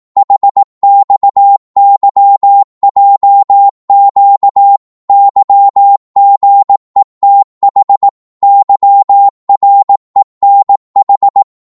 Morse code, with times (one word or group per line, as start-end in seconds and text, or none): 0.2-4.8 HXYJQ
5.1-8.1 YGETH
8.4-11.4 YRENH